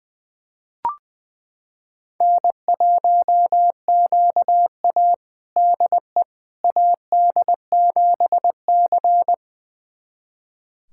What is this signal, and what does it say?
Morse code, heard as a keyed tone, about 20 wpm, 700 Hz. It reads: N1QA DE AD7C
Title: USA Callsign DE Callsign - 4 Characters - 20wpm